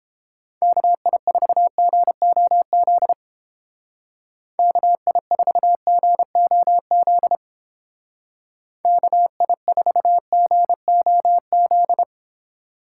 Morse code, heard as a keyed tone, about 29 wpm, 700 Hz. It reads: KI4GOZ KI4GOZ KI4GOZ